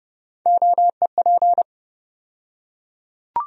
Morse code, heard as a keyed tone, about 31 wpm, 700 Hz. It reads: OEP
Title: USA Call Sign Postfixes - 3 Letters 30wpm